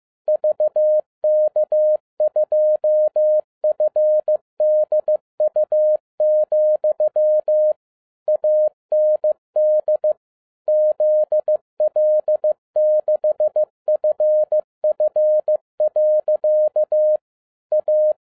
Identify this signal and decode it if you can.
Morse code, heard as a keyed tone, about 15 wpm, 610 Hz. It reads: VK2FDU, AND ZL6FF. A
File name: 2011-10-mp3 / n12-JL3YMV 15wpm